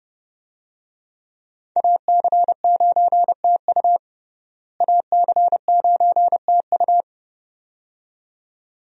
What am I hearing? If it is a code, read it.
Morse code, heard as a keyed tone, about 30 wpm, 700 Hz. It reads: AC9TU AC9TU
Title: USA Call Signs - 5 Characters - Repeated 2x - 30wpm